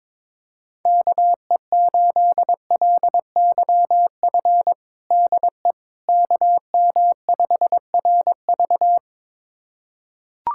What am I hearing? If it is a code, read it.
Morse code, heard as a keyed tone, about 22 wpm, 700 Hz. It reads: KE8LYF DE KM5RV
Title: USA Callsign DE Callsign 22wpm